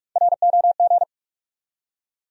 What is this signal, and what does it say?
Morse code, heard as a keyed tone, about 47 wpm, 700 Hz. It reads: ROG
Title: USA Common Names - Max 4 Letters - 45wpm